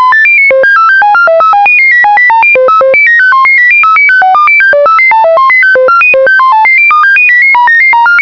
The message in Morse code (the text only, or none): none